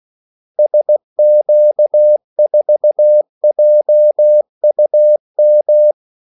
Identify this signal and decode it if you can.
Morse code, heard as a keyed tone, about 16 wpm, 600 Hz. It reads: SQ4JUM
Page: Kaldesignaler 16 wpm